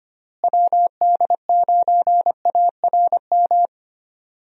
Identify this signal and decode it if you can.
Morse code, heard as a keyed tone, about 25 wpm, 700 Hz. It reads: WD9ARM
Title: USA Call Signs - 6 Characters - Repeated 2x - 25wpm